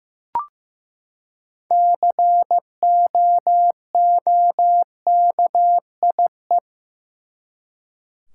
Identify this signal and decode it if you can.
Morse code, heard as a keyed tone, about 15 wpm, 700 Hz. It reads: COOKIE
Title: Six Letter Words - 15wpm